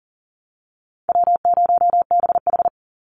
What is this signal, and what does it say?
Morse code, heard as a keyed tone, about 40 wpm, 700 Hz. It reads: W0BH
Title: Easy Call Signs Repeated 2x - 40wpm